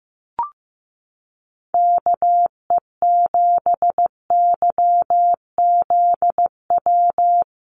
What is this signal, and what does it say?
Morse code, heard as a keyed tone, about 15 wpm, 700 Hz. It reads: KE7YZW
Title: USA Call Signs 15wpm